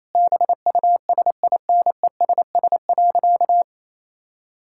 Morse code, heard as a keyed tone, about 27 wpm, 700 Hz. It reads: BUSINESS.
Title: Sentences from Top 1000 Words - Review Part 6 - 28wpm